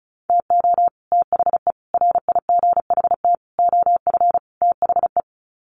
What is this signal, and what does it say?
Morse code, heard as a keyed tone, about 34 wpm, 700 Hz. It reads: TO THE RIGHT OF THE
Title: Sets of 5 Words Encore 35wpm